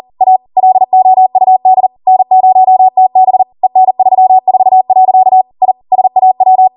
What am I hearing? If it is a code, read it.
Morse code, heard as a keyed tone, about 40 wpm, 750 Hz. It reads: A LOUD N0TB EN34. I SAW